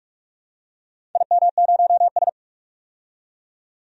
Morse code, heard as a keyed tone, about 44 wpm, 700 Hz. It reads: IM0S
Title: DXCC-100 Call Signs - 4 Characters - 45wpm